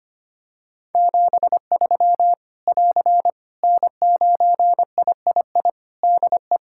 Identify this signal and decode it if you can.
Morse code, heard as a keyed tone, about 25 wpm, 700 Hz. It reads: 73 <AR> N9III DE
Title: Practice QSO - Exchange 4 - Ending - 25wpm